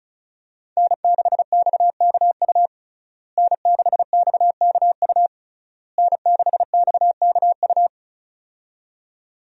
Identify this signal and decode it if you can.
Morse code, heard as a keyed tone, about 36 wpm, 700 Hz. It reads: N6XKU N6XKU N6XKU
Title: USA Call Signs - 5 Characters - Repeated 3x - 35wpm